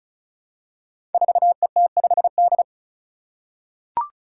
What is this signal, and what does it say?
Morse code, heard as a keyed tone, about 36 wpm, 700 Hz. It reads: 4ET5D
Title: Five Challenging Letter-Number Combinations 35wpm